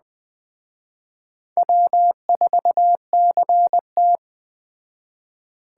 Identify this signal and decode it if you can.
Morse code, heard as a keyed tone, about 20 wpm, 700 Hz. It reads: W4CT
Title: Speed Racing - USA 1x2 Call Signs 17wpm